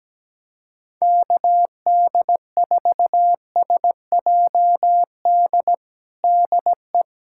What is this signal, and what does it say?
Morse code, heard as a keyed tone, about 17 wpm, 700 Hz. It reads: KD4SJD DE
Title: USA Callsign DE Callsign 17wpm